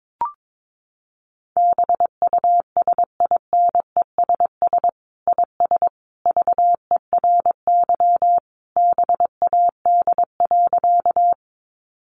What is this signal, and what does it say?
Morse code, heard as a keyed tone, about 22 wpm, 700 Hz. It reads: BUSINESS IS VERY BAD.
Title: Sentences from Top 700 Words 22wpm